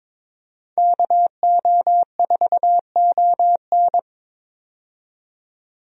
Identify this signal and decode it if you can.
Morse code, heard as a keyed tone, about 21 wpm, 700 Hz. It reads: KO4ON
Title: USA Call Signs - 5 Characters - 22wpm